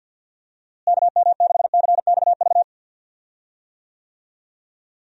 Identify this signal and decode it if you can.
Morse code, heard as a keyed tone, about 50 wpm, 700 Hz. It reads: KM6CXV
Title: USA Call Signs - 6 Characters - Repeated 3x - 50wpm